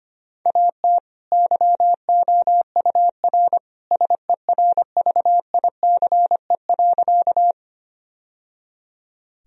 Morse code, heard as a keyed tone, about 25 wpm, 700 Hz. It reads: AT YOUR SERVICE.
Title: Sentences from Top 1000 Words - Review Part 4 - 25wpm